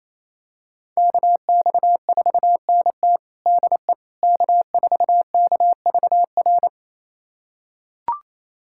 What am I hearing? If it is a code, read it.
Morse code, heard as a keyed tone, about 27 wpm, 700 Hz. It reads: KX4NT DE K4KVR
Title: USA Callsign DE Callsign - 5 Characters - 28wpm